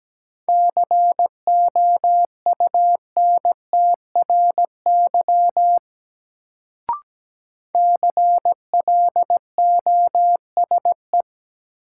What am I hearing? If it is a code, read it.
Morse code, heard as a keyed tone, about 17 wpm, 700 Hz. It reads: COUNTRY CLOSE
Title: Sentences from Top 300 Words - New Word Drill - 17wpm